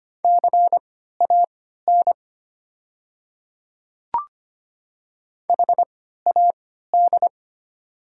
Morse code, heard as a keyed tone, about 25 wpm, spaced as for 14 wpm, 700 Hz. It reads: CAN HAD
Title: 3x Character Spacing - Top 100 Words - 25wpm